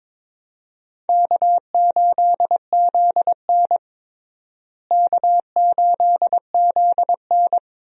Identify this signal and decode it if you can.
Morse code, heard as a keyed tone, about 22 wpm, 700 Hz. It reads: K8ZN K8ZN
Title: USA Call Signs - 4 Characters - Repeated 2x - 22wpm